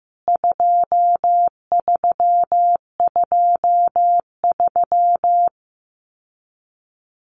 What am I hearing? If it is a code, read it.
Morse code, heard as a keyed tone, about 15 wpm, 700 Hz. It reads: 2323
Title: CWT Exchange 15wpm